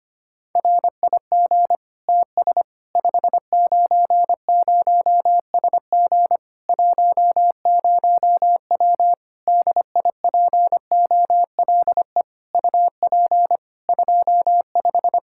Morse code, heard as a keyed tone, about 25 wpm, 700 Hz. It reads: RIG TS 590SG 10W DIPOLE UP 25
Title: K1USN SST QSO Practice 25wpm